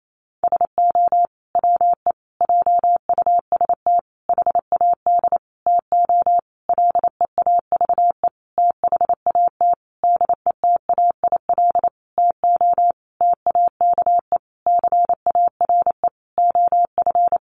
Morse code, heard as a keyed tone, about 28 wpm, 700 Hz. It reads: SO WE JUST HAD TO LEAVE THAT DETAIL TO TAKE CARE OF
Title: Sentences from Top 1500 Words 28wpm